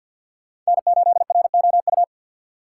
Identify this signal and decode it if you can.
Morse code, heard as a keyed tone, about 62 wpm, 700 Hz. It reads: N8ROU
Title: Speed Racing - USA 1x3 Call Signs 50wpm